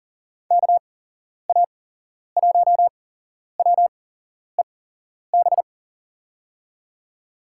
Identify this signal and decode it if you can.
Morse code, heard as a keyed tone, about 40 wpm, spaced as for 11 wpm, 700 Hz. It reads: KA1WEB
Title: USA Call Signs - 6 Characters - 8x Character Spacing - 40wpm